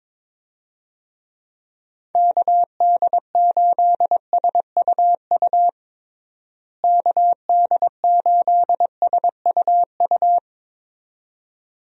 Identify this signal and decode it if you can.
Morse code, heard as a keyed tone, about 22 wpm, 700 Hz. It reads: KD8SUU KD8SUU
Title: USA Call Signs - 6 Characters - Repeated 2x - 22wpm